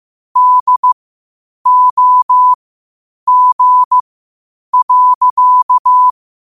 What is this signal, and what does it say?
Morse code, heard as a keyed tone, about 15 wpm, spaced as for 8 wpm, 1000 Hz. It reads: DOG.
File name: qotdfiles / 8wpm